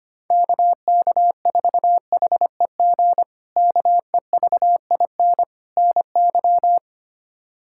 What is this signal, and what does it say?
Morse code, heard as a keyed tone, about 25 wpm, 700 Hz. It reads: KK4HEG KEVIN NY
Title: K1USN SST QSO Practice - 25wpm